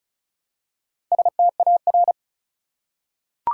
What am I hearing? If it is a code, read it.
Morse code, heard as a keyed tone, about 34 wpm, 700 Hz. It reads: STAR
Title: Four Letter Words - 35wpm